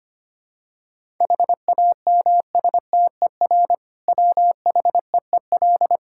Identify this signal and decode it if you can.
Morse code, heard as a keyed tone, about 25 wpm, 700 Hz. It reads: HAMSTER WHEEL
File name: practice / 25wpm / Difficult Words with Hints - 25wpm